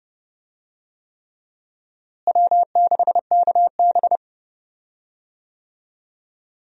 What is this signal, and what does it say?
Morse code, heard as a keyed tone, about 32 wpm, 700 Hz. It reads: W6KB
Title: USA Call Signs - 4 Characters - Repeated 3x - 30wpm